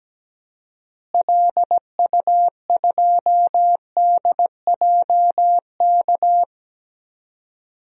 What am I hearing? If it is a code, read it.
Morse code, heard as a keyed tone, about 17 wpm, 700 Hz. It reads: LU2DJK
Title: DXCC-100 Call Signs - 6 Characters - Repeated 3x - 17wpm